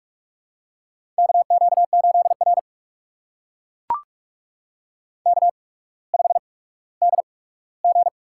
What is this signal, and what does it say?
Morse code, heard as a keyed tone, about 44 wpm, spaced as for 19 wpm, 690 Hz. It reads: KQ8R K5DG
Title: USA Call Signs - 4 Characters - 8x Character Spacing - 45wpm